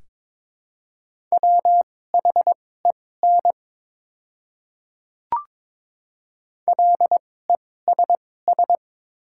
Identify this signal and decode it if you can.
Morse code, heard as a keyed tone, about 23 wpm, spaced as for 16 wpm, 700 Hz. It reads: WHEN LESS
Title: 2x Character Spacing - Top 500 Words - 22wpm